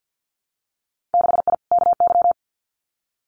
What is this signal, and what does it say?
Morse code, heard as a keyed tone, about 50 wpm, 700 Hz. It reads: 6I KY
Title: Winter Field Day Exchanges 50wpm